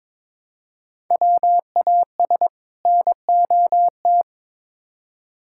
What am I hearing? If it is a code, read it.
Morse code, heard as a keyed tone, about 22 wpm, 700 Hz. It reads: WAS NOT
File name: Build A Phrase - 2-Word Phrases - 22wpm